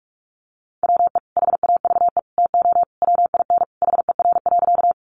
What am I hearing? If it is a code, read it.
Morse code, heard as a keyed tone, about 44 wpm, 700 Hz. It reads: WE HAVE TO WIN HER.
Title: Sentences from Top 1500 Words 45wpm